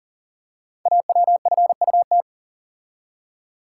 Morse code, heard as a keyed tone, about 38 wpm, 690 Hz. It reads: AWFUT
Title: Code Groups - 5 Letters - 40wpm